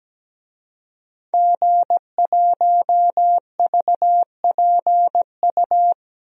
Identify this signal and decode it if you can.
Morse code, heard as a keyed tone, about 17 wpm, 700 Hz. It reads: G1VPU
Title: DXCC-100 Call Signs - 5 Characters - Repeated 3x - 17wpm